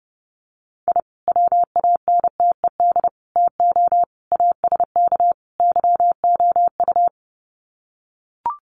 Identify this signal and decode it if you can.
Morse code, heard as a keyed tone, about 30 wpm, 700 Hz. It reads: I WANTED TO ASK YOU
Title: Sets of 5 Words Encore2 30wpm